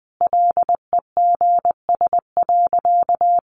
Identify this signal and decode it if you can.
Morse code, heard as a keyed tone, about 19 wpm, 700 Hz. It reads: LEGS.
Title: Additional Easy Phrases 20wpm